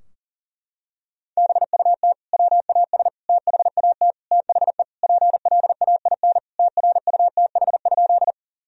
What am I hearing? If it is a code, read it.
Morse code, heard as a keyed tone, about 40 wpm, 700 Hz. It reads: BUT WAS THAT THE PLAIN TRUTH?
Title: Sentences from Top 1000 Words - Review Part 2 - 40wpm